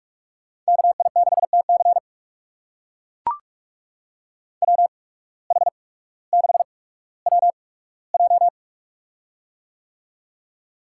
Morse code, heard as a keyed tone, about 42 wpm, spaced as for 19 wpm, 700 Hz. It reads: KI6TC WH6WJ
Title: USA Call Signs - 5 Characters - 8x Character Spacing - 45wpm